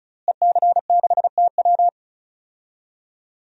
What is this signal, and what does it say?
Morse code, heard as a keyed tone, about 36 wpm, 700 Hz. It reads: EC6TW
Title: DXCC-100 Call Signs - 5 Characters - Repeated 3x - 35wpm